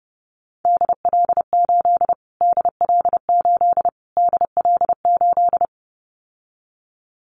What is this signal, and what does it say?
Morse code, heard as a keyed tone, about 30 wpm, 700 Hz. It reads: DL8 DL8 DL8
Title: Callsign Prefixes for Top 25 DXCC - Repeated 3x - 30wpm